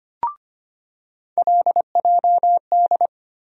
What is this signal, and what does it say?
Morse code, heard as a keyed tone, about 25 wpm, 700 Hz. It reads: LJD
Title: USA Call Sign Postfixes - 3 Letters 25wpm